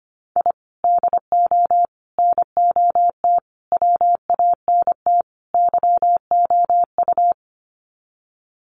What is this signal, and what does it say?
Morse code, heard as a keyed tone, about 25 wpm, 700 Hz. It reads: I DO NOT WANT YOU
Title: Sets of 5 Words 25wpm